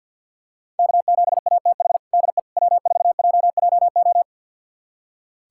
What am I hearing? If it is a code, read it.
Morse code, heard as a keyed tone, about 48 wpm, 700 Hz. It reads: K7ATH DE W4JJO
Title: USA Callsign DE Callsign - 5 Characters - 50wpm